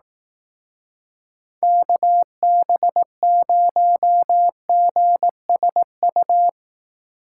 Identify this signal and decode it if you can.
Morse code, heard as a keyed tone, about 18 wpm, 700 Hz. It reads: KB0GSU
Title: Speed Racing - USA 2x3 Call Signs 15wpm